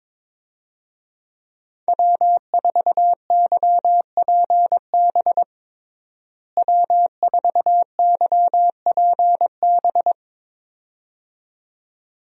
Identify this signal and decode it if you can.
Morse code, heard as a keyed tone, about 22 wpm, 700 Hz. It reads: W4YPB W4YPB
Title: USA Call Signs - 5 Characters - Repeated 2x - 22wpm